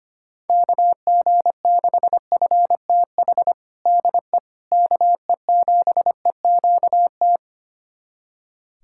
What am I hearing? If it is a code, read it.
Morse code, heard as a keyed tone, about 25 wpm, 700 Hz. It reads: KG6FTH DE KE7EQT
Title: USA Callsign DE Callsign - 6 Characters - 25wpm